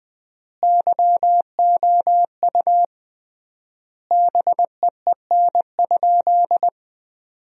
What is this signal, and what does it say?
Morse code, heard as a keyed tone, about 20 wpm, 700 Hz. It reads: YOU BEEN?
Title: 3x Word-Spacing - Sentences from Top 400 Words - 20wpm